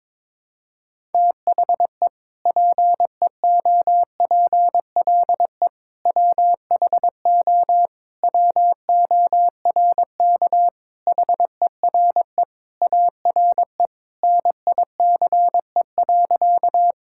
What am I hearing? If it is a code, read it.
Morse code, heard as a keyed tone, about 22 wpm, 700 Hz. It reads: THE PEOPLE WHO WORK HERE ARE NICE.
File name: practice / 22wpm / Sentences with 3-Word Phrases - 22wpm